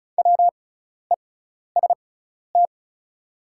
Morse code, heard as a keyed tone, about 36 wpm, spaced as for 12 wpm, 690 Hz. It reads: WEST
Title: 6x Character Spacing - Top 500 Words - 35wpm